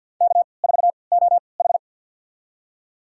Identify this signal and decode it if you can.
Morse code, heard as a keyed tone, about 53 wpm, spaced as for 29 wpm, 700 Hz. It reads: K4OH
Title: USA Call Signs - 4 Characters - 3x Character Spacing - 50wpm